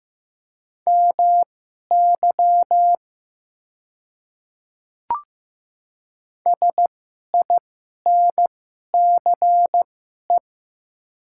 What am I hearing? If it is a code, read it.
Morse code, heard as a keyed tone, about 15 wpm, spaced as for 11 wpm, 700 Hz. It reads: MY SINCE
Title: 2x Character Spacing - Top 300 Words - 15wpm